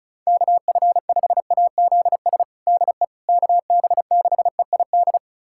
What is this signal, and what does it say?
Morse code, heard as a keyed tone, about 34 wpm, 700 Hz. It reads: KF5AZS DE KB6EID
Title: USA Callsign DE Callsign - 6 Characters - 35wpm